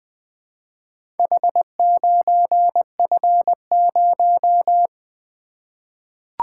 Morse code, heard as a keyed tone, about 20 wpm, 700 Hz. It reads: H9F0
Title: Four Letter-Number Combinations - 20wpm